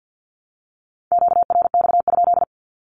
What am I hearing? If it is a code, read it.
Morse code, heard as a keyed tone, about 48 wpm, 700 Hz. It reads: QRX?
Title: Q-Codes Repeated 3x 50wpm